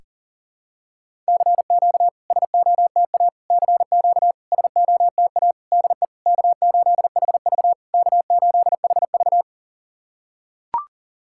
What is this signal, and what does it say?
Morse code, heard as a keyed tone, about 40 wpm, 700 Hz. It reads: CQ SOTA CQ SOTA DE K8HV K8HV
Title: SOTA QSO Practice 40wpm